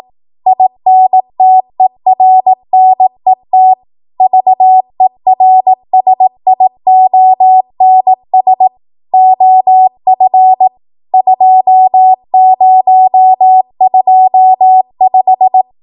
Morse code, heard as a keyed tone, about 18 wpm, 750 Hz. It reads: INTERNET VERSIONS OF 2025
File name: Morse / 18WPM